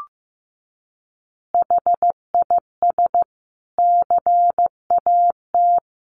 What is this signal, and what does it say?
Morse code, heard as a keyed tone, about 15 wpm, 700 Hz. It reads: HIS CAT
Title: Additional Easy Phrases 15wpm